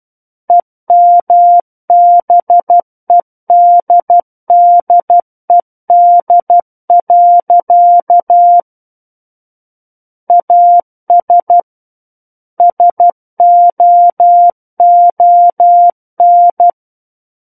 Morse code, heard as a keyed tone, about 12 wpm, 700 Hz. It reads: EMBEDDED. AS SOON
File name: Book One / 03-Chapter 3 - 12 WPM